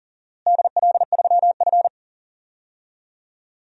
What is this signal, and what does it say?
Morse code, heard as a keyed tone, about 40 wpm, 700 Hz. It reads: DL3F
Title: DXCC-100 Call Signs - 4 Characters - Repeated 2x - 40wpm